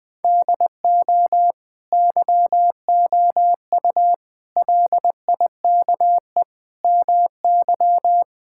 Morse code, heard as a keyed tone, about 21 wpm, 700 Hz. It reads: DO YOU LIKE MY
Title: Sentences from Top 1000 Words - Review Part 5 - 20wpm